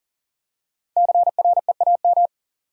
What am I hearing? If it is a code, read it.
Morse code, heard as a keyed tone, about 37 wpm, 700 Hz. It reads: CREAM
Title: Five Letter Words - 40wpm